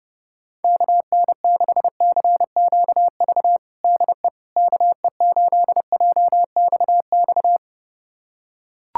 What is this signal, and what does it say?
Morse code, heard as a keyed tone, about 30 wpm, 700 Hz. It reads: KN6CQV DE KE8JXX
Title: USA Callsign DE Callsign - 6 Characters - 30wpm